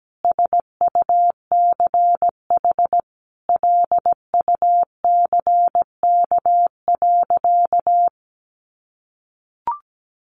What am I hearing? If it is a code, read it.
Morse code, heard as a keyed tone, about 17 wpm, 700 Hz. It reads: SUCH LUCK.
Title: Sentences from Top 1500 Words 17wpm